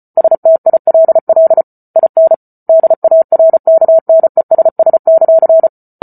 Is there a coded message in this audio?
Yes: Morse code from 0.2 to 5.7 s, heard as a keyed tone, about 34 wpm, 650 Hz. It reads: STILL IN DARKNESS;